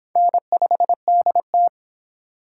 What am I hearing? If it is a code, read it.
Morse code, heard as a keyed tone, about 26 wpm, 700 Hz. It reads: N5DT
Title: Speed Racing - USA 1x2 Call Signs 22wpm